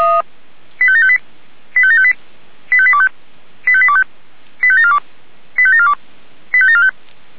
Five-tone selective call (ZVEI), heard as quick runs of five tones, tones of about 70 ms. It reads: 85758 85758 85825 85825 85742 85742 85745